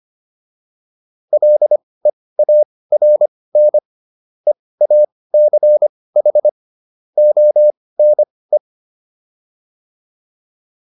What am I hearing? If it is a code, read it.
Morse code, heard as a keyed tone, about 25 wpm, spaced as for 18 wpm, 600 Hz. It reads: LEARN EACH ONE